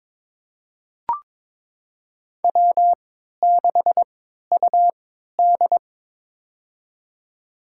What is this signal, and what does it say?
Morse code, heard as a keyed tone, about 21 wpm, spaced as for 12 wpm, 700 Hz. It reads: W6UD
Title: USA Call Signs - 4 Characters - 3x Character Spacing - 22wpm